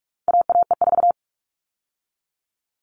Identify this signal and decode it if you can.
Morse code, heard as a keyed tone, about 44 wpm, 700 Hz. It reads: AAE4
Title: All Cut Numbers - 4-Digit 45wpm